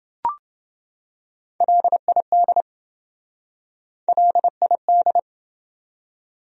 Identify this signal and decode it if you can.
Morse code, heard as a keyed tone, about 27 wpm, 700 Hz. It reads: LID LID